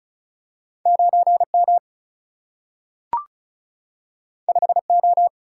Morse code, heard as a keyed tone, about 35 wpm, 700 Hz. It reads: 9M 5O
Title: Single-Letter Single-Number 35wpm